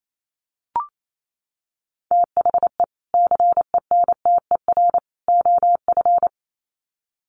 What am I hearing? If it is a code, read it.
Morse code, heard as a keyed tone, about 28 wpm, 700 Hz. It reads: THE CENTER OF
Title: Sets of 3 Words 28wpm